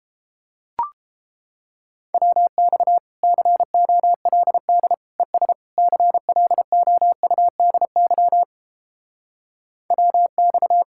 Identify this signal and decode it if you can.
Morse code, heard as a keyed tone, about 31 wpm, 700 Hz. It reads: WX COLD ES CLOUDY WX